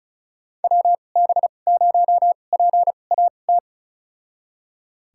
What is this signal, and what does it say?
Morse code, heard as a keyed tone, about 36 wpm, spaced as for 26 wpm, 690 Hz. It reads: WB0PAT
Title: USA Call Signs - 6 Characters - 2x Character Spacing - 35wpm